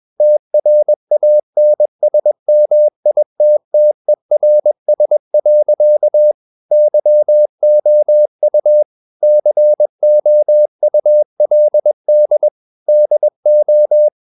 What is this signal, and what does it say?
Morse code, heard as a keyed tone, about 21 wpm, 600 Hz. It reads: TRANSMITTERS. YOU COULD DO
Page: CW med Gnister